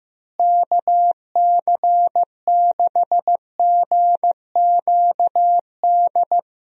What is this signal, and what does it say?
Morse code, heard as a keyed tone, about 15 wpm, 700 Hz. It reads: KC6GQD
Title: USA Call Signs - 6 Characters - Repeated 2x - 15wpm